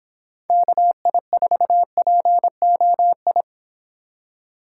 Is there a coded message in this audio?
Yes: Morse code from 0.5 to 3.4 s, heard as a keyed tone, about 25 wpm, 700 Hz. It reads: KI4POI